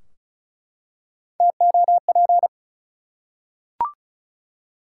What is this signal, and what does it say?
Morse code, heard as a keyed tone, about 34 wpm, 700 Hz. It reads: TOP
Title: Three Letter Words - 35wpm